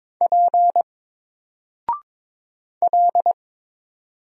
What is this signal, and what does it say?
Morse code, heard as a keyed tone, about 22 wpm, 700 Hz. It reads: P L
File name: Single-Letter-Number-22wpm